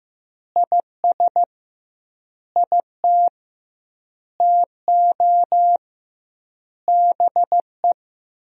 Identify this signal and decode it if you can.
Morse code, heard as a keyed tone, about 15 wpm, 700 Hz. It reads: IS IT TO BE
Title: 2x Word-Spacing - Sentences from Top 300 Words - 15wpm